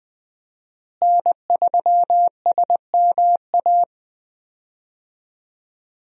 Morse code, heard as a keyed tone, about 21 wpm, 700 Hz. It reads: N3SMA
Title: USA Call Signs - 5 Characters - Repeated 2x - 20wpm